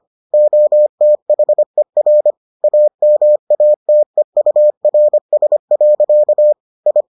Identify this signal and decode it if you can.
Morse code, heard as a keyed tone, about 25 wpm, 600 Hz. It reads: OTHER AMATEURS. I